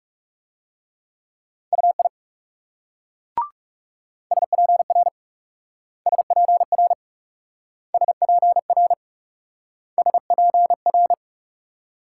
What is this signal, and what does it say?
Morse code, heard as a keyed tone, about 37 wpm, 700 Hz. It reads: UI SPR SPR SPR SPR
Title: Speed-Racing Multi-letter Phonemes - 30wpm